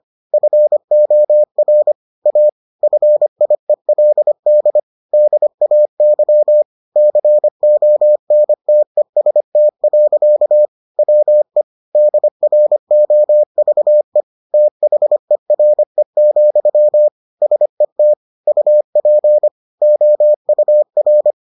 Morse code, heard as a keyed tone, about 25 wpm, 600 Hz. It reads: FOR A FIELD DAY CONTEST. WE DROVE THERE, SET UP OUR